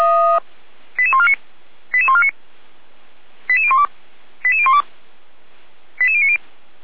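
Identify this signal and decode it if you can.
Five-tone selective call (ZVEI), heard as quick runs of five tones, tones of about 70 ms. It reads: 88149 88149 88012 88012 88090
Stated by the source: ZVEI five-tone selective call received on SDR, from the Feuerwehr Orlamünde audio server.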